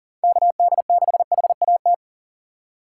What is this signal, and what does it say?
Morse code, heard as a keyed tone, about 39 wpm, 700 Hz. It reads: KD6HAT
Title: USA Call Signs - 6 Characters - 4x Character Spacing - 40wpm